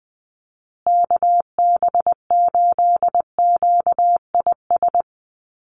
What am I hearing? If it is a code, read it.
Morse code, heard as a keyed tone, about 19 wpm, 700 Hz. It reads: KB8QIS
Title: USA Call Signs Repeated 3x 20wpm